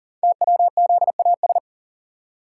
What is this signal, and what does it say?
Morse code, heard as a keyed tone, about 42 wpm, 690 Hz. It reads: TWZAS
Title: Code Groups - 5 Letters - 40wpm